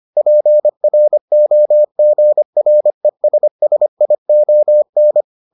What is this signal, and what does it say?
Morse code, heard as a keyed tone, about 25 wpm, 600 Hz. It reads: PROGRESSION